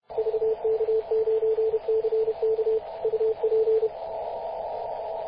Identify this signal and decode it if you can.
Morse code, heard as a keyed tone, about 32 wpm, 440 Hz, with radio noise. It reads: VK9CK UP
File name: posts / 2016-mar-19-0241z-vk9ck-21-032mhz-cw